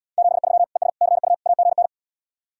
Morse code, heard as a keyed tone, about 77 wpm, 700 Hz. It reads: 73 ES CU AGN